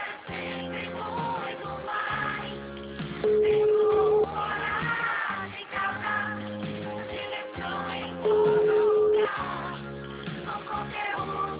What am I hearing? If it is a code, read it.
Ringback tone (Europe, 425 Hz), with noise.
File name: out- -2061- - - .74725